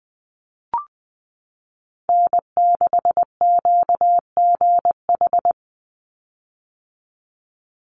Morse code, heard as a keyed tone, about 20 wpm, 700 Hz. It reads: N6QGH
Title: USA Call Signs 20wpm